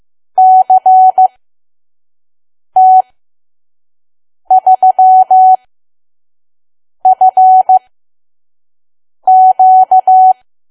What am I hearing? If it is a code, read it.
Morse code, heard as a keyed tone, about 15 wpm, spaced as for 5 wpm, 750 Hz. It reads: CT3FQ
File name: Archive / 5 WPM files / PM